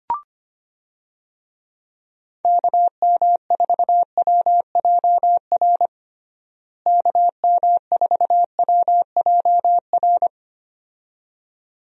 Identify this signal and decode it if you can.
Morse code, heard as a keyed tone, about 25 wpm, 700 Hz. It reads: KM4WJR KM4WJR
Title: USA Call Signs - 6 Characters - Repeated 2x - 25wpm